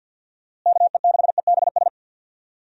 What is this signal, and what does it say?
Morse code, heard as a keyed tone, about 51 wpm, 700 Hz. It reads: KE6EBS
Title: USA Call Signs - 6 Characters - Repeated 3x - 50wpm